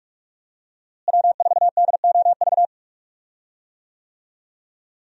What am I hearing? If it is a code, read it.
Morse code, heard as a keyed tone, about 45 wpm, 700 Hz. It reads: W4DOV
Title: USA Call Signs - 5 Characters - Repeated 3x - 45wpm